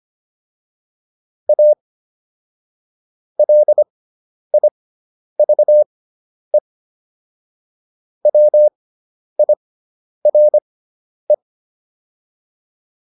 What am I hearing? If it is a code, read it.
Morse code, heard as a keyed tone, about 25 wpm, spaced as for 10 wpm, 600 Hz. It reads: A LIVE WIRE